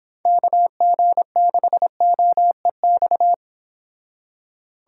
Morse code, heard as a keyed tone, about 25 wpm, 700 Hz. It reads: KG6OEX